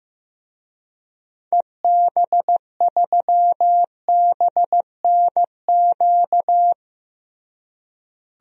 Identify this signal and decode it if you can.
Morse code, heard as a keyed tone, about 15 wpm, 700 Hz. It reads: EB3BNQ
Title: DXCC-100 Call Signs - 6 Characters - Repeated 2x - 15wpm